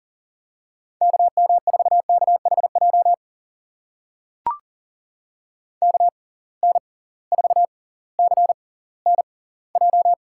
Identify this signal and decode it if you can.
Morse code, heard as a keyed tone, about 40 wpm, spaced as for 20 wpm, 700 Hz. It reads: KM4KHJ KN4CNJ
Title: USA Call Signs - 6 Characters - 6x Character Spacing - 40wpm